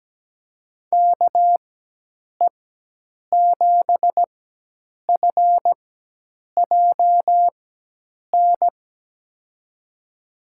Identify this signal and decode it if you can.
Morse code, heard as a keyed tone, about 17 wpm, 700 Hz. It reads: K E 7 F J N
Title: USA Call Signs - 6 Characters - 4x Character Spacing - 17wpm